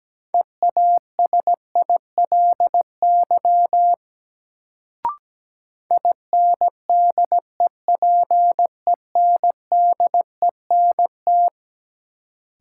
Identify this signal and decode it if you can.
Morse code, heard as a keyed tone, about 17 wpm, 700 Hz. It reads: EASILY INDEPENDENT
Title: Sentences from Top 900 Words - New Word Drill - 17wpm